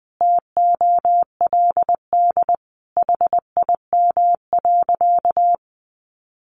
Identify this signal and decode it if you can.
Morse code, heard as a keyed tone, about 20 wpm, 700 Hz. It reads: TOLD HIM.
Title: Sentences from Top 1200 Words 20wpm